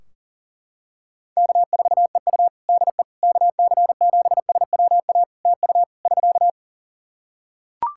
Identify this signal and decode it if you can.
Morse code, heard as a keyed tone, about 40 wpm, 700 Hz. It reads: K4EU DE KC7SWA TU <SK>
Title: Practice QSO - Exchange 4 - Ending - 40wpm